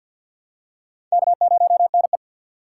Morse code, heard as a keyed tone, about 46 wpm, 700 Hz. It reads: K0NE
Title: USA Call Signs - 2x Character Spacing - 50wpm